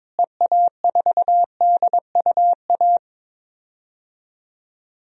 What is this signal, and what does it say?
Morse code, heard as a keyed tone, about 22 wpm, 700 Hz. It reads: EA4DUA
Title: DXCC-100 Call Signs - 6 Characters - Repeated 3x - 22wpm